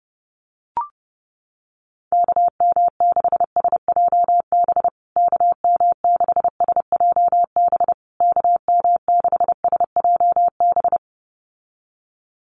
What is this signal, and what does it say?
Morse code, heard as a keyed tone, about 30 wpm, 700 Hz. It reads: KM6SJB KM6SJB KM6SJB
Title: USA Call Signs Repeated 3x 30wpm